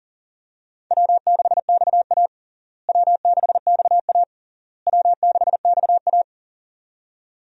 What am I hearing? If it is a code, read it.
Morse code, heard as a keyed tone, about 40 wpm, 700 Hz. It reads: W6XA W6XA W6XA
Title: USA Call Signs - 4 Characters - Repeated 3x - 40wpm